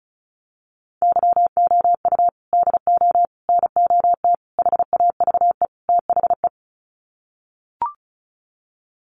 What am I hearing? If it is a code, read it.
Morse code, heard as a keyed tone, about 34 wpm, 700 Hz. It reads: YOU DO NOT HAVE THE
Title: Sets of 5 Words 35wpm